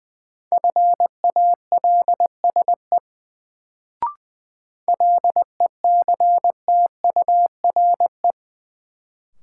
Morse code, heard as a keyed tone, about 20 wpm, 700 Hz. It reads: FALSE LECTURE
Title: Sentences from Top 1600 Words - New Word Drill - 20wpm